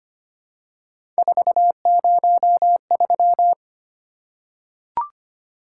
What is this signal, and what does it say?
Morse code, heard as a keyed tone, about 25 wpm, 700 Hz. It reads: 403
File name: Three Letter-Number Combinations - 25wpm